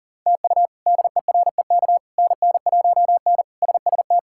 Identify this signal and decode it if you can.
Morse code, heard as a keyed tone, about 40 wpm, 700 Hz. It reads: TU DEREK NN1N SST
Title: K1USN SST QSO Practice - 40wpm